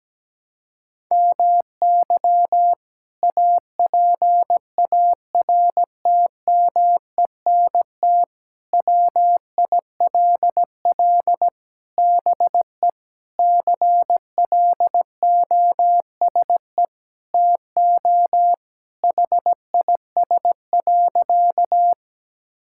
Morse code, heard as a keyed tone, about 17 wpm, 700 Hz. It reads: MY APARTMENT WILL BE CLOSE TO HIS.
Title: Sentences from Top 1000 Words - Review Part 6 - 17wpm